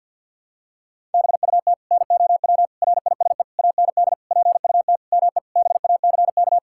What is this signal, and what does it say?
Morse code, heard as a keyed tone, about 49 wpm, 700 Hz. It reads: BUT NOW RISE AND PUT ME BACK
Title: Sentences from Top 1000 Words - Review Part 6 - 50wpm